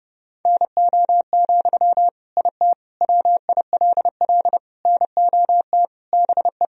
Morse code, heard as a keyed tone, about 30 wpm, 700 Hz. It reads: NO, IT WILL NOT BE
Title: Sentences from Top 1000 Words - Review Part 4 - 30wpm